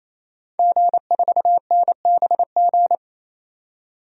Morse code, heard as a keyed tone, about 28 wpm, 700 Hz. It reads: G4NBG
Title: DXCC-100 Call Signs - 5 Characters - 28wpm